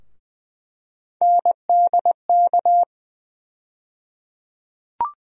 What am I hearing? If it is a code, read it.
Morse code, heard as a keyed tone, about 20 wpm, 700 Hz. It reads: NDK
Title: USA Call Sign Postfixes - 3 Letters 20wpm